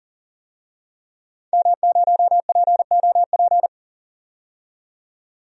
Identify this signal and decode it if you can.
Morse code, heard as a keyed tone, about 40 wpm, 700 Hz. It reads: M0POP
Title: DXCC-100 Call Signs - 5 Characters - 40wpm